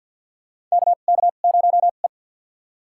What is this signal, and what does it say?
Morse code, heard as a keyed tone, about 53 wpm, spaced as for 37 wpm, 700 Hz. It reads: KK0E
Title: USA Call Signs - 4 Characters - 2x Character Spacing - 50wpm